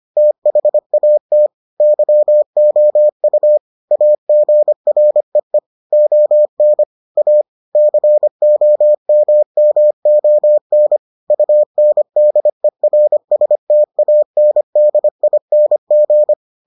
Morse code, heard as a keyed tone, about 25 wpm, 600 Hz. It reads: THAT YOU AGREE ON A COMMON UNDERSTANDING